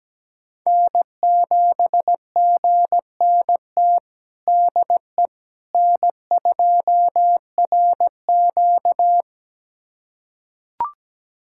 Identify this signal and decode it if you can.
Morse code, heard as a keyed tone, about 17 wpm, 700 Hz. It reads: N7GNT DE N2RQ
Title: USA Callsign DE Callsign 17wpm